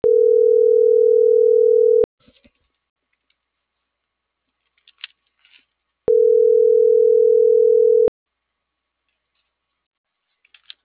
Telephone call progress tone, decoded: ringback tone

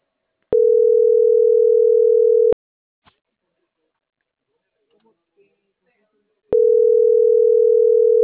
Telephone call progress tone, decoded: ringback tone

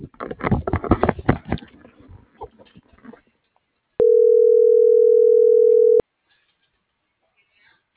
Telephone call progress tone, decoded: ringback tone

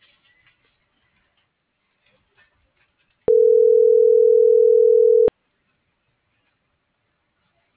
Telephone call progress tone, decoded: ringback tone

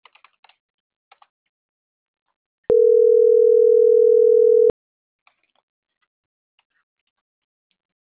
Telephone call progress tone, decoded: ringback tone

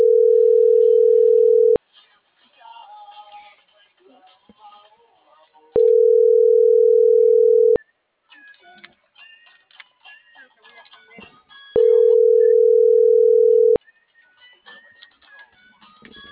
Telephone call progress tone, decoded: ringback tone

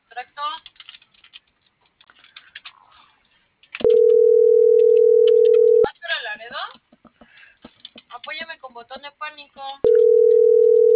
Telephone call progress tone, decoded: ringback tone

